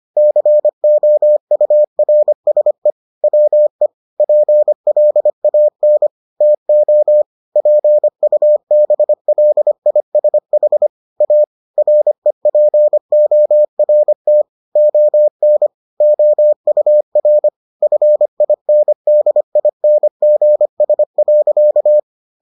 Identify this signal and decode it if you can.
Morse code, heard as a keyed tone, about 25 wpm, 600 Hz. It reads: COURSE WE PLAN TO PUBLISH A REPORT ON OUR FINDINGS.